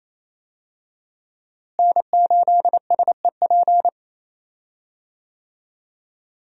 Morse code, heard as a keyed tone, about 28 wpm, 700 Hz. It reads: N8SEP
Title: USA Call Signs - 5 Characters - Repeated 3x - 28wpm